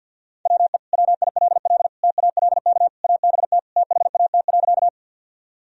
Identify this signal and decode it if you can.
Morse code, heard as a keyed tone, about 46 wpm, 700 Hz. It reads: WE WILL TALK ABT THAT.
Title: Sentences from Top 400 Words and Abbreviations 50wpm